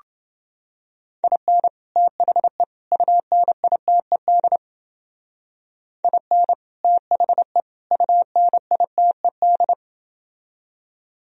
Morse code, heard as a keyed tone, about 30 wpm, 700 Hz. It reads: IN THE UNITED IN THE UNITED